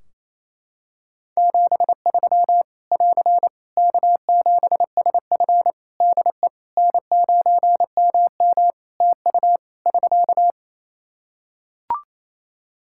Morse code, heard as a keyed tone, about 28 wpm, 700 Hz. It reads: 73 <AR> K7SF DE N9MM TU <SK>
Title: K1USN SST QSO Practice 28wpm